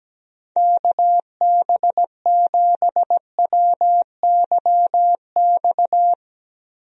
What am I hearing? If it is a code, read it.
Morse code, heard as a keyed tone, about 17 wpm, 700 Hz. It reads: KB7WYX
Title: USA Call Signs - 6 Characters - Repeated 2x - 17wpm